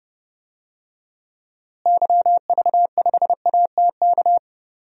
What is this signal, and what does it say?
Morse code, heard as a keyed tone, about 30 wpm, 700 Hz. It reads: YV5ATK
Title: DXCC-100 Call Signs - 6 Characters - Repeated 3x - 30wpm